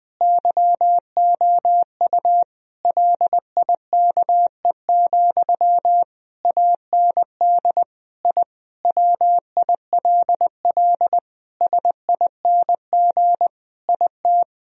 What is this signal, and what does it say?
Morse code, heard as a keyed tone, about 20 wpm, 700 Hz. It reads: YOU LIKE, AND I WILL SING IT
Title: Sentences from Top 1000 Words - Review Part 2 - 20wpm